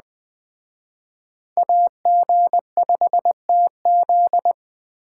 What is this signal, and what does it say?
Morse code, heard as a keyed tone, about 20 wpm, 700 Hz. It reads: AG5TZ
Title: Speed Racing - USA 2x2 Call Signs 17wpm